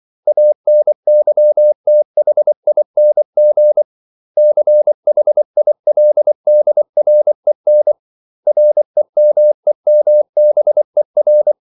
Morse code, heard as a keyed tone, about 24 wpm, 600 Hz. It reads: ANYTHING CHILDREN REMEMBER
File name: CW / Practice / 500MostCommonEnglishWords24